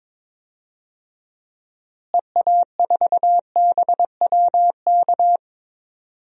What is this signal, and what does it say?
Morse code, heard as a keyed tone, about 21 wpm, 700 Hz. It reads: EA4BWK
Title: DXCC-100 Call Signs - 6 Characters - Repeated 2x - 22wpm